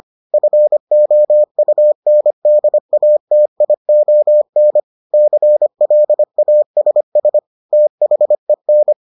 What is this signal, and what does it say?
Morse code, heard as a keyed tone, about 25 wpm, 600 Hz. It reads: FOUNDATION CLASS THEN